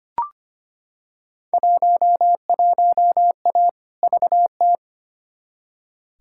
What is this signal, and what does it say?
Morse code, heard as a keyed tone, about 25 wpm, 700 Hz. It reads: 11A VT
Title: Field Day Exchanges Encore - 25wpm